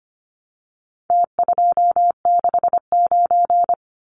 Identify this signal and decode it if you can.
Morse code, heard as a keyed tone, about 25 wpm, 700 Hz. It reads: T269
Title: Numbers - Contest Style - 4-Digit 25wpm